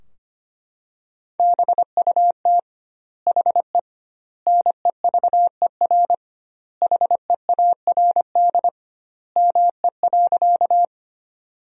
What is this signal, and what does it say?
Morse code, heard as a keyed tone, about 25 wpm, 700 Hz. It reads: BUT HE NEVER HEARD ME.
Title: 2x Word-Spacing - Sentences from Top 400 Words - 25wpm